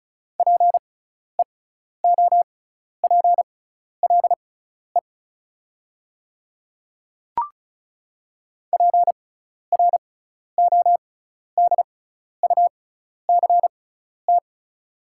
Morse code, heard as a keyed tone, about 34 wpm, spaced as for 12 wpm, 690 Hz. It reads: PEOPLE PRODUCT
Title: 6x Character Spacing - Top 500 Words - 35wpm